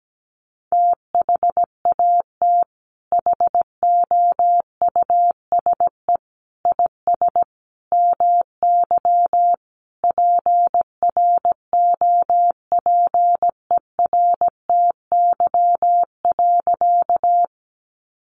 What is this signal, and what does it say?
Morse code, heard as a keyed tone, about 17 wpm, 700 Hz. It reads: THAT HOUSE IS MY PROPERTY.
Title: Sentences from Top 1000 Words 17wpm